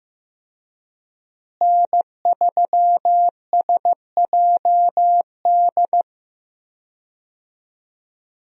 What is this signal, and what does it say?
Morse code, heard as a keyed tone, about 16 wpm, 700 Hz. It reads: N3SJD
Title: USA Call Signs - 5 Characters - Repeated 3x - 15wpm